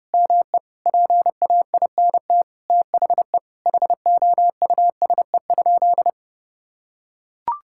Morse code, heard as a keyed tone, about 31 wpm, 700 Hz. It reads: ME PAINT THE HOUSE?
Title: Sentences from Top 1000 Words - Review Part 6 - 30wpm